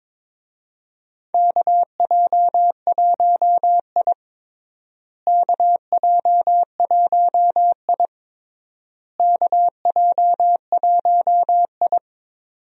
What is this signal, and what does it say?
Morse code, heard as a keyed tone, about 22 wpm, 700 Hz. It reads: KJ1I KJ1I KJ1I
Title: USA Call Signs - 4 Characters - Repeated 3x - 22wpm